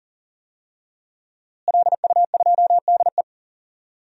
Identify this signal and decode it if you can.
Morse code, heard as a keyed tone, about 40 wpm, 700 Hz. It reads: LU2DE
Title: DXCC-100 Call Signs - 5 Characters - Repeated 3x - 40wpm